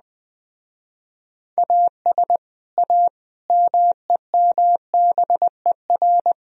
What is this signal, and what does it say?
Morse code, heard as a keyed tone, about 20 wpm, 700 Hz. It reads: AS A MEMBER